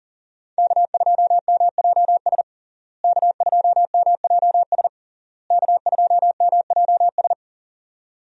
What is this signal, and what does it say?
Morse code, heard as a keyed tone, about 40 wpm, 700 Hz. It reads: K2MJS K2MJS K2MJS
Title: USA Call Signs - 5 Characters - Repeated 3x - 40wpm